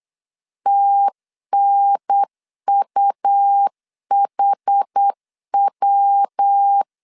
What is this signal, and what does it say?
Morse code, heard as a keyed tone, about 8 wpm, 800 Hz. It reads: TNUHW
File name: CW / DET / 1 3 1500C S B 8wpm 4141